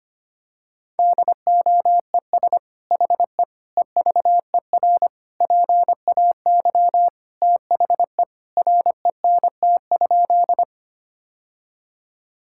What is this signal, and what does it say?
Morse code, heard as a keyed tone, about 25 wpm, 700 Hz. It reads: DOES HE EVER PAY THE RENT?
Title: Sentences from Top 1000 Words - Review Part 5 - 25wpm